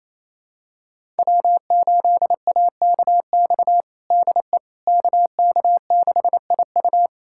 Morse code, heard as a keyed tone, about 28 wpm, 700 Hz. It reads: W8AKX DE KK6IU
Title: USA Callsign DE Callsign - 5 Characters - 28wpm